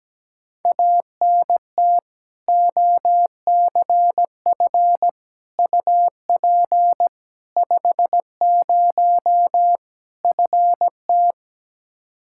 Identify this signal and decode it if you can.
Morse code, heard as a keyed tone, about 17 wpm, 700 Hz. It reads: ANT OCF UP 50 FT
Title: Practice QSO - Exchange 2 - Info - 17wpm